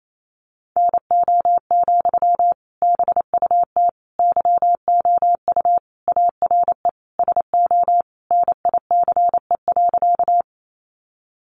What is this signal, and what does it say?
Morse code, heard as a keyed tone, about 28 wpm, 700 Hz. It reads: NO, BUT YOU ARE SO NICE.
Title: Sentences from Top 1100 Words 28wpm